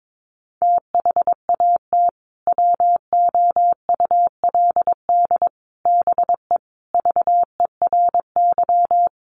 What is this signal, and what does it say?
Morse code, heard as a keyed tone, about 22 wpm, 700 Hz. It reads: THAT WOULD BE VERY
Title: Sentences from Top 200 Words 22wpm